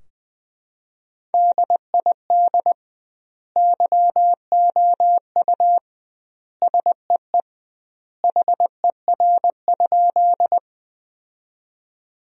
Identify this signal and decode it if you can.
Morse code, heard as a keyed tone, about 21 wpm, 700 Hz. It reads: DID YOU SEE HER?
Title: 2x Word-Spacing - Sentences from Top 100 Words - 20wpm